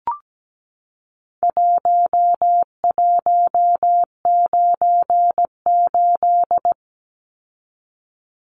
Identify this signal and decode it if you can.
Morse code, heard as a keyed tone, about 17 wpm, 700 Hz. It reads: 1198
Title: Numbers - Contest Style - 4-Digit 17wpm